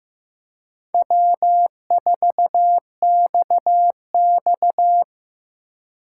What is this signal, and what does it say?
Morse code, heard as a keyed tone, about 15 wpm, 700 Hz. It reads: W4XX
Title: USA Call Signs - 4 Characters - 2x Character Spacing - 15wpm